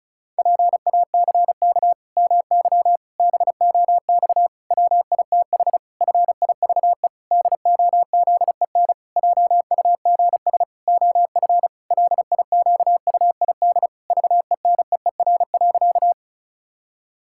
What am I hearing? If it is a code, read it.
Morse code, heard as a keyed tone, about 36 wpm, 700 Hz. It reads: PACK MY BOX WITH FIVE DOZEN JUGS OF LIQUID VENEER.
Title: Pangram Challenge v2 - 35wpm